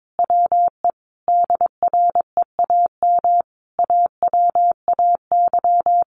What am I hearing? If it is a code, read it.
Morse code, heard as a keyed tone, about 22 wpm, 700 Hz. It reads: WE DREAM AWAY
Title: Sentences from Top 1500 Words 22wpm